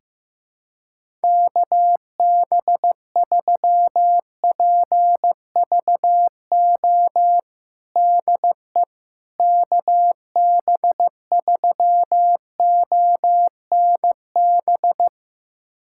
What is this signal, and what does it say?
Morse code, heard as a keyed tone, about 15 wpm, 700 Hz. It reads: KB3PVO DE KB3ONB
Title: USA Callsign DE Callsign - 6 Characters - 15wpm